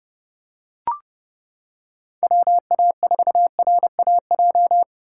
Morse code, heard as a keyed tone, about 29 wpm, 700 Hz. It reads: WA4RAJ